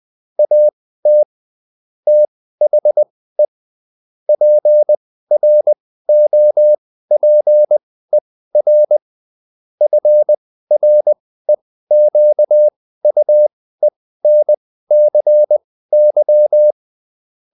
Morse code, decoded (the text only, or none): AT THE PROPER FREQUENCY